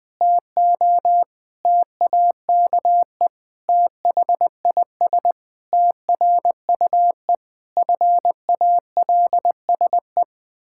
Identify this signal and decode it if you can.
Morse code, heard as a keyed tone, about 21 wpm, 700 Hz. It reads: TO TAKE THIS TRUE FALSE
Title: Pangram Challenge v2 - 20wpm